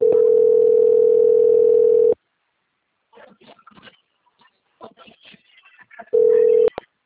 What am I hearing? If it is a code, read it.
Ringback tone (North America), 440 + 480 Hz.